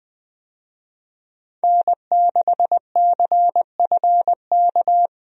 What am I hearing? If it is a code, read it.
Morse code, heard as a keyed tone, about 20 wpm, 700 Hz. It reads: N6CFK
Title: USA Call Signs - 5 Characters - Repeated 3x - 20wpm